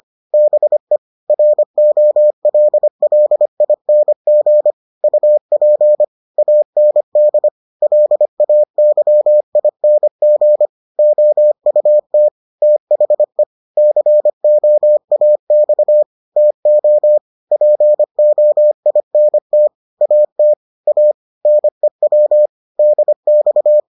Morse code, heard as a keyed tone, about 25 wpm, 600 Hz. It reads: BE ROLLING UP AND LAYING OUT THE COAX TO POINT AT A NEW DX